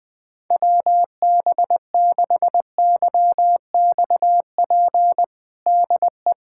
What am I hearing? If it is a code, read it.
Morse code, heard as a keyed tone, about 20 wpm, 700 Hz. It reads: WB6YXP DE
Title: USA Callsign DE Callsign 20wpm